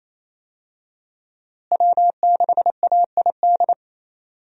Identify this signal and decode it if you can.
Morse code, heard as a keyed tone, about 28 wpm, 700 Hz. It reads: W6AID
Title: USA Call Signs - 5 Characters - Repeated 3x - 28wpm